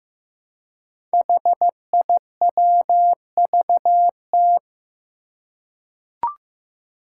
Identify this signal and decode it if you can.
Morse code, heard as a keyed tone, about 15 wpm, 700 Hz. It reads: HIWVT
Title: Five Challenging Letter-Number Combinations 15wpm